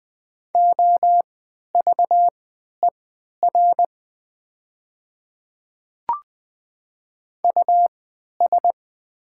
Morse code, heard as a keyed tone, about 20 wpm, spaced as for 11 wpm, 700 Hz. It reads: OVER US
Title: 3x Character Spacing - Top 200 Words - 20wpm